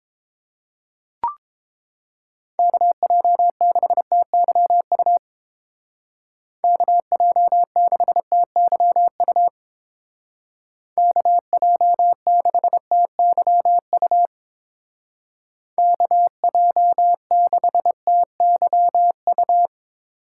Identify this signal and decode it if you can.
Morse code, heard as a keyed tone, about 30 wpm, 700 Hz. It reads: KJ6TYU KJ6TYU KJ6TYU KJ6TYU